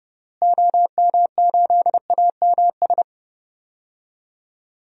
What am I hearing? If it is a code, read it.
Morse code, heard as a keyed tone, about 30 wpm, 700 Hz. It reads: OM8AMS
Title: DXCC-100 Call Signs - 6 Characters - Repeated 3x - 30wpm